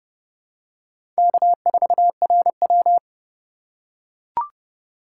Morse code, heard as a keyed tone, about 30 wpm, 700 Hz. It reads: K4RW
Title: USA Call Signs - 4x Character Spacing - 30wpm